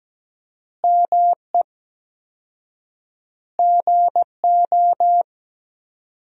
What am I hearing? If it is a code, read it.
Morse code, heard as a keyed tone, about 17 wpm, 700 Hz. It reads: ME GO
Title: 4x Word-Spacing - Sentences from Top 300 Words - 17wpm